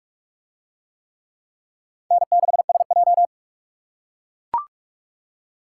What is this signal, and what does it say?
Morse code, heard as a keyed tone, about 44 wpm, 700 Hz. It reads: N6SJ